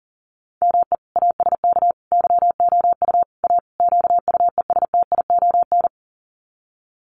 Morse code, heard as a keyed tone, about 40 wpm, 700 Hz. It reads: ME ASK YOU A QUESTION
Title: Sets of 5 Words Encore2 40wpm